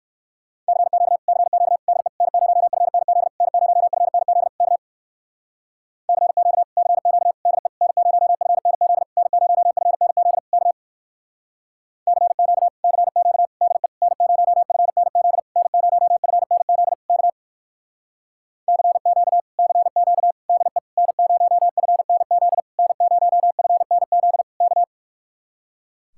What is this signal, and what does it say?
Morse code, heard as a keyed tone, about 55 wpm, 700 Hz. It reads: CQ CQ DE N0FNZ N0FNZ K CQ CQ DE N0FNZ N0FNZ K CQ CQ DE N0FNZ N0FNZ K CQ CQ DE N0FNZ N0FNZ K
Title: Speed-Racing Common QSO Phrases Encore - 45wpm